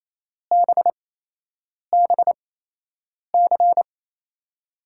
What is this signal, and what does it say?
Morse code, heard as a keyed tone, about 28 wpm, spaced as for 8 wpm, 700 Hz. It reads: BBC
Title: USA Call Signs - 5 Characters - 8x Character Spacing - 28wpm